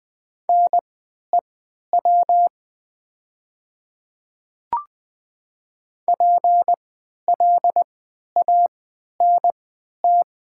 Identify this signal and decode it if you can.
Morse code, heard as a keyed tone, about 20 wpm, spaced as for 11 wpm, 700 Hz. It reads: NEW PLANT
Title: 3x Character Spacing - Top 300 Words - 20wpm